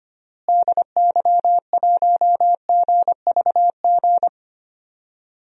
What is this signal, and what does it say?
Morse code, heard as a keyed tone, about 25 wpm, 700 Hz. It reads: DY1GVG
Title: DXCC-100 Call Signs - 6 Characters - Repeated 2x - 25wpm